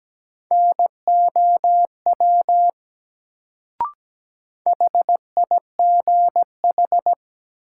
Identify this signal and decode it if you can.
Morse code, heard as a keyed tone, about 17 wpm, 700 Hz. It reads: NOW HIGH
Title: Sets of 2 Words - New Word Drill - 17wpm